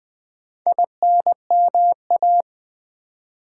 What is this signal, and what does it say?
Morse code, heard as a keyed tone, about 20 wpm, 690 Hz. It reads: INMA
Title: Build A Word - 2 Letters At A Time - 6-Letter Words - 20wpm